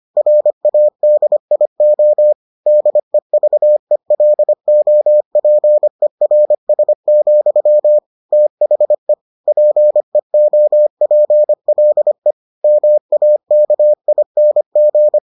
Morse code, heard as a keyed tone, about 25 wpm, 600 Hz. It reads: RADIO DEVELOPERS, THE PEOPLE MAKING